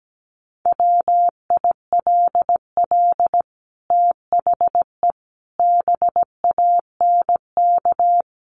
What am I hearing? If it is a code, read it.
Morse code, heard as a keyed tone, about 17 wpm, 700 Hz. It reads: WILL THE BANK
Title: Sentences from Top 1500 Words 17wpm